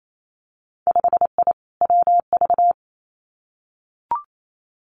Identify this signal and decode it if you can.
Morse code, heard as a keyed tone, about 28 wpm, 700 Hz. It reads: 5I WV
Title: Winter Field Day Exchanges Encore 28wpm